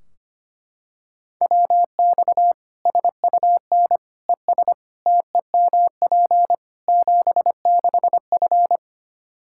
Morse code, heard as a keyed tone, about 25 wpm, 700 Hz. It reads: WX SUN ES TEMP 76F
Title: Practice QSO - Exchange 2 - Info - 25wpm